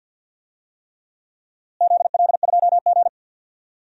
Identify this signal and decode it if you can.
Morse code, heard as a keyed tone, about 50 wpm, 700 Hz. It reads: ZL2G
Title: DXCC-100 Call Signs - 4 Characters - Repeated 3x - 50wpm